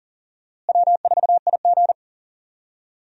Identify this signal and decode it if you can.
Morse code, heard as a keyed tone, about 38 wpm, 700 Hz. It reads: W4IG
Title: USA Call Signs - 4 Characters - Repeated 3x - 40wpm